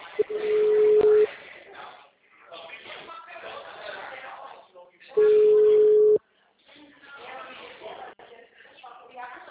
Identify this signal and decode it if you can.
Ringback tone (Europe, 425 Hz), with speech.